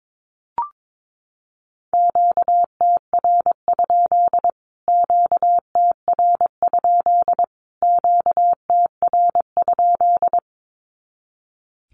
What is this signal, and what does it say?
Morse code, heard as a keyed tone, about 22 wpm, 700 Hz. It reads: QTR? QTR? QTR?
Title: Q-Codes Repeated 3x 22wpm